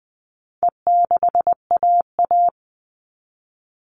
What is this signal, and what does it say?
Morse code, heard as a keyed tone, about 20 wpm, 700 Hz. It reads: E6AA
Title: All Cut Numbers - 4-Digit 20wpm